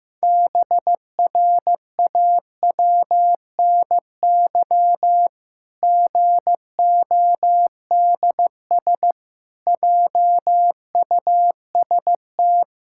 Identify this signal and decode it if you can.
Morse code, heard as a keyed tone, about 15 wpm, 700 Hz. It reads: BRAWNY GODS JUST
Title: Pangram Challenge v2 - 15wpm